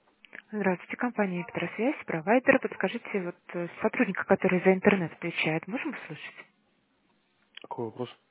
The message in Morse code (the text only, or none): none